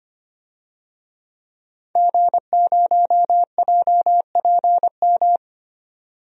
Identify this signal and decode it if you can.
Morse code, heard as a keyed tone, about 25 wpm, 700 Hz. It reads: G0JPM
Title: DXCC-100 Call Signs - 5 Characters - 25wpm